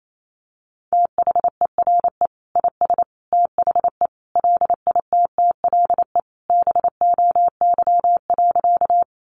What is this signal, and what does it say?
Morse code, heard as a keyed tone, about 28 wpm, 700 Hz. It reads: THERE IS THE LITTLE BOY.
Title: Sentences from Top 200 Words 28wpm